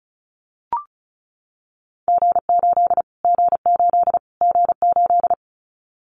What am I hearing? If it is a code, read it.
Morse code, heard as a keyed tone, about 34 wpm, 700 Hz. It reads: G8 G8 G8
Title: Callsign Prefixes for Top 25 DXCC - Repeated 3x - 35wpm